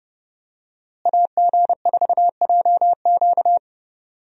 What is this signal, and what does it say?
Morse code, heard as a keyed tone, about 31 wpm, 700 Hz. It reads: AG4JQ
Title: USA Call Signs - 5 Characters - Repeated 2x - 30wpm